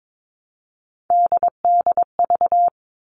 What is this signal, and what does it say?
Morse code, heard as a keyed tone, about 22 wpm, 700 Hz. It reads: DDV
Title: All Cut Numbers - 3-Digit 22wpm